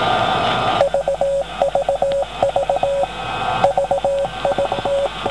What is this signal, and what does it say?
Morse code, heard as a keyed tone, about 18 wpm, 550 Hz, with radio noise. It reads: VVV VIT